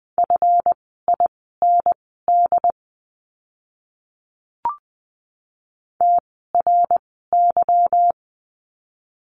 Morse code, heard as a keyed tone, about 20 wpm, spaced as for 15 wpm, 700 Hz. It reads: FIND TRY
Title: 160 Lesson 26f - Letter g - Words 20wpm